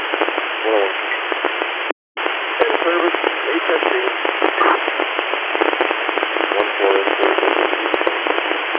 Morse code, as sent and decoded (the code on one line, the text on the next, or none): none